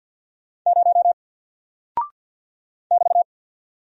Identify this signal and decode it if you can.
Morse code, heard as a keyed tone, about 48 wpm, 700 Hz. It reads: 0 <BT>
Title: Single Character (Letter:Number:Punctuation:Prosign) Alt 50wpm